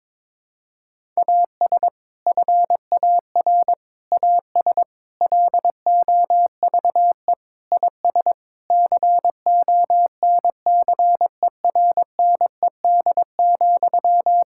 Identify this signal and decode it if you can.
Morse code, heard as a keyed tone, about 22 wpm, 700 Hz. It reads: AS FAR AS LOVE IS CONCERNED,
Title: Sentences from Top 1000 Words - Review Part 6 - 22wpm